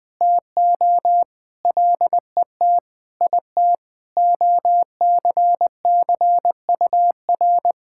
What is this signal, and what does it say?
Morse code, heard as a keyed tone, about 20 wpm, 700 Hz. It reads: TO LET IT OCCUR
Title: Sentences from Top 1000 Words - Review Part 6 - 20wpm